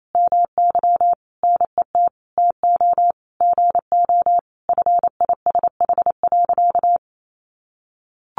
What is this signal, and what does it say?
Morse code, heard as a keyed tone, about 28 wpm, 700 Hz. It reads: MY NET TO GO FISH.
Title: Sentences from Top 1400 Words 28wpm